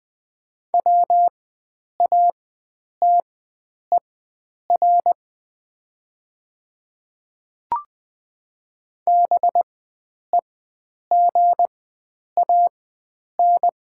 Morse code, heard as a keyed tone, about 20 wpm, spaced as for 9 wpm, 700 Hz. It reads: WATER BEGAN
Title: 4x Character Spacing - Top 500 Words - 20wpm